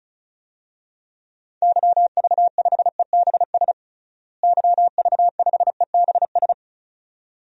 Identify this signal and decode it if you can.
Morse code, heard as a keyed tone, about 36 wpm, 700 Hz. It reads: YV5EBS YV5EBS
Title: DXCC-100 Call Signs - 6 Characters - Repeated 2x - 35wpm